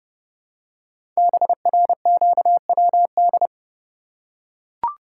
Morse code, heard as a keyed tone, about 30 wpm, 700 Hz. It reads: BRQWD
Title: Code Groups - 5 Letters - 30wpm